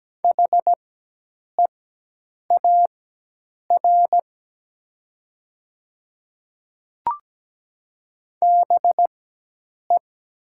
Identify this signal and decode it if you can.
Morse code, heard as a keyed tone, about 17 wpm, spaced as for 8 wpm, 700 Hz. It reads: HEAR BE
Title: 4x Character Spacing - Top 300 Words - 17wpm